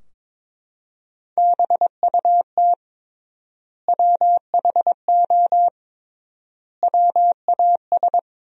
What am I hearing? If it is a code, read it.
Morse code, heard as a keyed tone, about 22 wpm, 700 Hz. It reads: BUT WHO WAS
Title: 3x Word-Spacing - Sentences from Top 200 Words - 22wpm